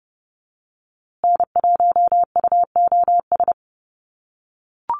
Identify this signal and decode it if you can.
Morse code, heard as a keyed tone, about 30 wpm, 700 Hz. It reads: N1UOS
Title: USA Call Signs Repeated 3x 30wpm